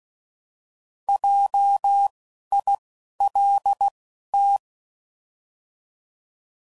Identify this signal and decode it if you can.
Morse code, heard as a keyed tone, about 16 wpm, spaced as for 12 wpm, 800 Hz. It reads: JILT